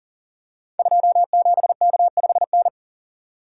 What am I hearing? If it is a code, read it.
Morse code, heard as a keyed tone, about 43 wpm, 700 Hz. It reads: 27K5N
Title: Code Groups - 5 Letter:Numbers - 40wpm